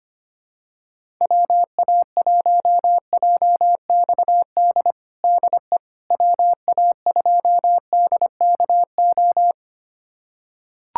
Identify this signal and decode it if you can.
Morse code, heard as a keyed tone, about 25 wpm, 700 Hz. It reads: WA1JXD DE WA2DKO
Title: USA Callsign DE Callsign - 6 Characters - 25wpm